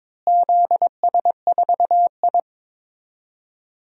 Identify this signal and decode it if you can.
Morse code, heard as a keyed tone, about 22 wpm, 700 Hz. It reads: ZS4I
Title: DXCC-100 Call Signs - 4 Characters - Repeated 2x - 22wpm